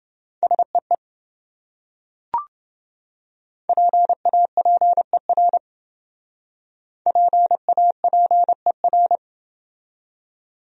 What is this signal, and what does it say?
Morse code, heard as a keyed tone, about 28 wpm, 700 Hz. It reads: SEE PAPER PAPER